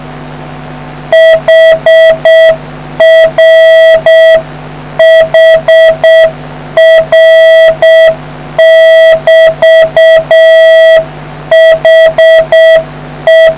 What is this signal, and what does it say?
Morse code, heard as a keyed tone, about 7 wpm, 650 Hz. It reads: HR HR<BT>HE